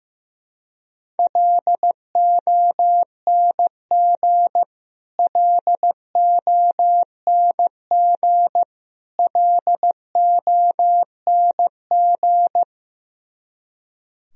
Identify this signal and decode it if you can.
Morse code, heard as a keyed tone, about 15 wpm, 700 Hz. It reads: LONG LONG LONG
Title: Top 100 Words Repeated 15wpm